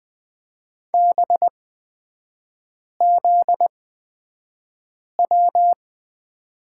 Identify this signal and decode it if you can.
Morse code, heard as a keyed tone, about 20 wpm, spaced as for 5 wpm, 700 Hz. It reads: BZW
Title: 243 Lesson 39e - Letter z - Three Characters 20wpm